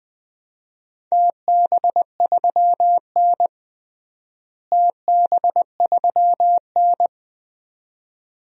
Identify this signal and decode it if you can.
Morse code, heard as a keyed tone, about 20 wpm, 700 Hz. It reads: TB3N TB3N
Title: DXCC-100 Call Signs - 4 Characters - Repeated 2x - 20wpm